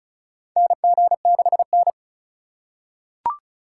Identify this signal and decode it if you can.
Morse code, heard as a keyed tone, about 36 wpm, 700 Hz. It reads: NG6N
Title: USA Call Signs - 4 Characters - 2x Character Spacing - 35wpm